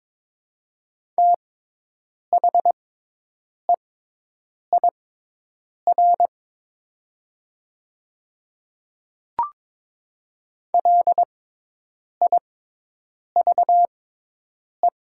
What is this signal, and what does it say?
Morse code, heard as a keyed tone, about 22 wpm, spaced as for 8 wpm, 700 Hz. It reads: THEIR LIVE
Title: 6x Character Spacing - Top 200 Words - 22wpm